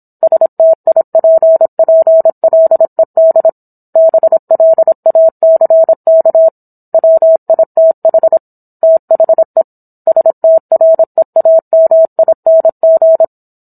Morse code, heard as a keyed tone, about 27 wpm, 650 Hz. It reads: STIPPLED BLACK WITH THE STREAMING
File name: Book One / 17-Chapter 17 - 26 WPM